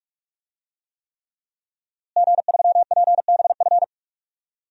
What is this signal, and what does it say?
Morse code, heard as a keyed tone, about 42 wpm, 700 Hz. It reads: G3PBF
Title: DXCC-100 Call Signs - 5 Characters - Repeated 3x - 45wpm